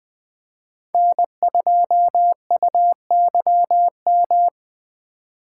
Morse code, heard as a keyed tone, about 20 wpm, 700 Hz. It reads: N2UYM
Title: Speed Racing - USA 1x3 Call Signs 17wpm